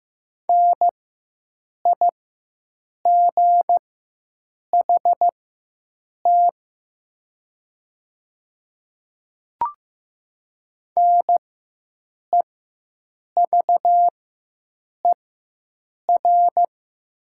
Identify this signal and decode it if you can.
Morse code, heard as a keyed tone, about 15 wpm, spaced as for 7 wpm, 700 Hz. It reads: NIGHT NEVER
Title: 4x Character Spacing - Top 300 Words - 15wpm